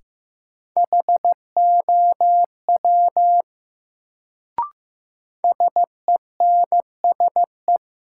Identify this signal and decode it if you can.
Morse code, heard as a keyed tone, about 15 wpm, 700 Hz. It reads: HOW SENSE
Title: Sets of 2 Words - New Word Drill - 15wpm